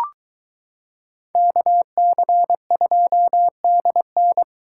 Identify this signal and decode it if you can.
Morse code, heard as a keyed tone, about 22 wpm, 700 Hz. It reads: KC2DN